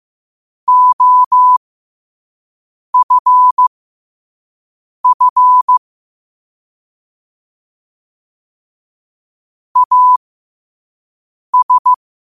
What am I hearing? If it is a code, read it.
Morse code, heard as a keyed tone, about 15 wpm, spaced as for 5 wpm, 1000 Hz. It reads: OFF AS
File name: qotdfiles / 5wpm